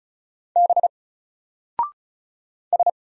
Morse code, heard as a keyed tone, about 35 wpm, 700 Hz. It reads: B S
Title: Single Character (Letter:Number:Punctuation:Prosign) 35wpm